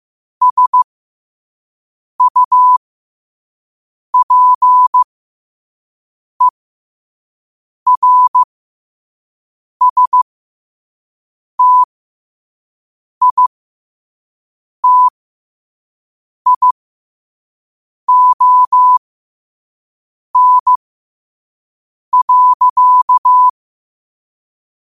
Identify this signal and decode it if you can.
Morse code, heard as a keyed tone, about 15 wpm, spaced as for 5 wpm, 1000 Hz. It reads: SUPERSTITION.
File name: qotdfiles / 5wpm